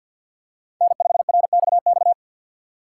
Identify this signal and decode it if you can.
Morse code, heard as a keyed tone, about 52 wpm, 700 Hz. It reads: N5RXX
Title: USA Call Signs - 3x Character Spacing - 50wpm